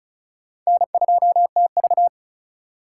Morse code, heard as a keyed tone, about 36 wpm, 700 Hz. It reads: N2TV
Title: USA Call Signs - 4 Characters - 2x Character Spacing - 35wpm